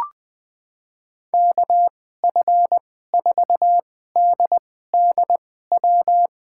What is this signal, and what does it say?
Morse code, heard as a keyed tone, about 20 wpm, spaced as for 14 wpm, 700 Hz. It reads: KF4DDW
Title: USA Call Signs - 6 Characters - 2x Character Spacing - 20wpm